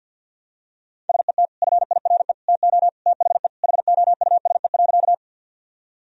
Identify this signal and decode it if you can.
Morse code, heard as a keyed tone, about 50 wpm, 700 Hz. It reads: SET FIRE TO THE HOUSE.
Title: Sentences from Top 1000 Words - Review Part 1 - 50wpm